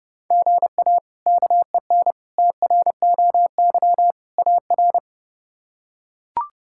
Morse code, heard as a keyed tone, about 30 wpm, 700 Hz. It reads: GA KEN TROY AR
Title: K1USN SST QSO Practice - 30wpm